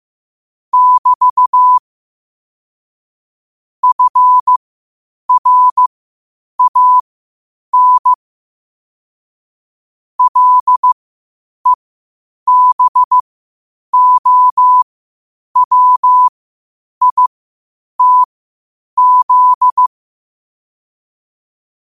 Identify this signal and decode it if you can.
Morse code, heard as a keyed tone, about 15 wpm, spaced as for 8 wpm, 1000 Hz. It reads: <BT> FRAN LEBOWITZ